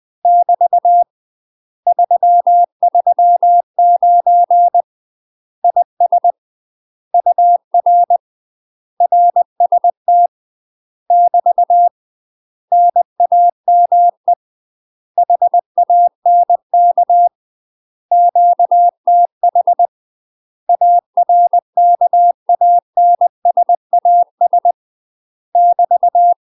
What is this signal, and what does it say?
Morse code, heard as a keyed tone, about 20 wpm, 700 Hz. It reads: <BT> 339 IS UR RST <BT> NAME HANK QTH ARKANSAS <BT>
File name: UK MASTER QSOs TRAINING / 20WPM / UK MASTER QSOs 0058